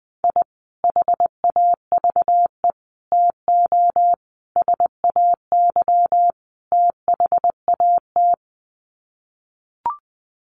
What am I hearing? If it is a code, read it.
Morse code, heard as a keyed tone, about 20 wpm, 700 Hz. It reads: I HAVE TO SAY THAT
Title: Sets of 5 Words Encore 20wpm